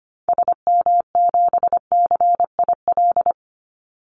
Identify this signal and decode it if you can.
Morse code, heard as a keyed tone, about 25 wpm, 700 Hz. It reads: SM7CIL
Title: Call Signs - 25wpm